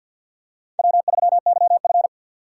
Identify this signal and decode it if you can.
Morse code, heard as a keyed tone, about 47 wpm, 690 Hz. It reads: W3YF
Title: USA Call Signs - 4x Character Spacing - 50wpm